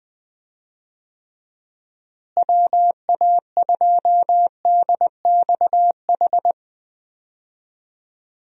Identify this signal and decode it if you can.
Morse code, heard as a keyed tone, about 21 wpm, 700 Hz. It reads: WA2DXH
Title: USA Call Signs - 6 Characters - Repeated 2x - 20wpm